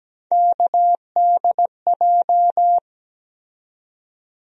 Morse code, heard as a keyed tone, about 17 wpm, 700 Hz. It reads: KDJ
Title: USA Call Sign Postfixes - 3 Letters 17wpm